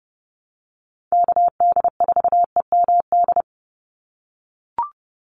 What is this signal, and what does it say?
Morse code, heard as a keyed tone, about 30 wpm, 700 Hz. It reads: KD4EMD
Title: USA Call Signs 30wpm